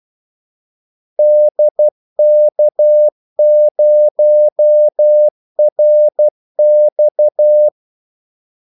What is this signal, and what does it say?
Morse code, heard as a keyed tone, about 12 wpm, 600 Hz. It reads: DK0RX
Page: 2 ekstra ordmellemrum mellem kaldesignalerne